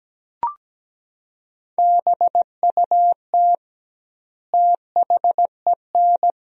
Morse code, heard as a keyed tone, about 17 wpm, 700 Hz. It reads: BUT THEN
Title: 2x Word-Spacing - Sentences from Top 300 Words - 17wpm